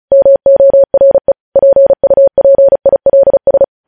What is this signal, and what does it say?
Morse code, heard as a keyed tone, about 36 wpm, 550 Hz. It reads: MORE PUPILS
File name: podcast / News headlines-1x-35x35